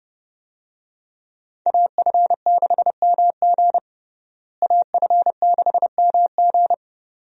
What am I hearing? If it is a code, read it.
Morse code, heard as a keyed tone, about 31 wpm, 700 Hz. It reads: AF6MG AF6MG
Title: USA Call Signs - 5 Characters - Repeated 3x - 30wpm